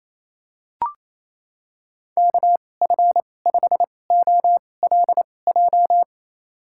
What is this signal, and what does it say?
Morse code, heard as a keyed tone, about 28 wpm, spaced as for 20 wpm, 700 Hz. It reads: KF5OLJ
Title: USA Call Signs - 6 Characters - 2x Character Spacing - 28wpm